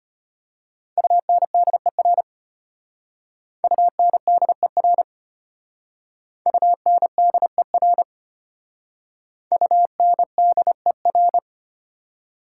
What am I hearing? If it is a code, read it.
Morse code, heard as a keyed tone, about 31 wpm, 690 Hz. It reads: UNDER UNDER UNDER UNDER
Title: Speed-Racing Prefixes and Suffixes 25wpm